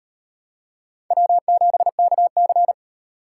Morse code, heard as a keyed tone, about 35 wpm, 700 Hz. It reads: W7KC